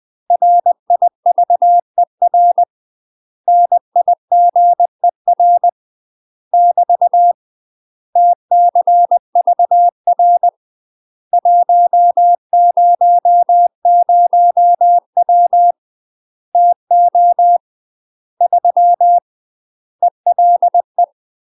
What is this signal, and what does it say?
Morse code, heard as a keyed tone, about 20 wpm, 700 Hz. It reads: RIVER NIGER <BT> TCVR 100W TO 3 ELE
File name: UK MASTER QSOs TRAINING / 20WPM / UK MASTER QSOs 0032